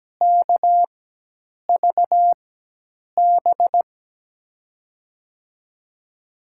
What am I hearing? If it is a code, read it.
Morse code, heard as a keyed tone, about 17 wpm, spaced as for 8 wpm, 700 Hz. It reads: KVB
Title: USA Call Signs - 6 Characters - 4x Character Spacing - 17wpm